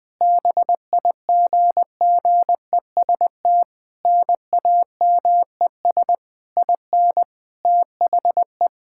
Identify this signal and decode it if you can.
Morse code, heard as a keyed tone, about 20 wpm, 700 Hz. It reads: BIGGEST NAMES IN THE
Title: Sentences with 7-Word Phrases - Spoken Beginning - 20wpm